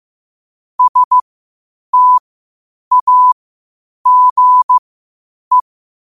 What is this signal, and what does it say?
Morse code, heard as a keyed tone, about 15 wpm, spaced as for 8 wpm, 1000 Hz. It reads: STAGE
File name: qotdfiles / 8wpm